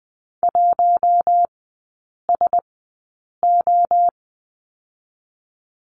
Morse code, heard as a keyed tone, about 20 wpm, spaced as for 8 wpm, 700 Hz. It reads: 1SO
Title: 087 Lesson 15e - Number 5 - Three Characters 20wpm